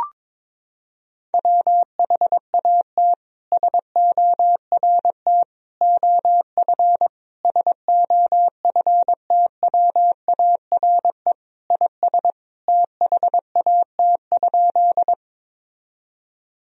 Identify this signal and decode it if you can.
Morse code, heard as a keyed tone, about 22 wpm, 700 Hz. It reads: WHAT SORT OF SOFTWARE IS THAT?
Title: Sentences from Top 1000 Words - Review Part 3 - 22wpm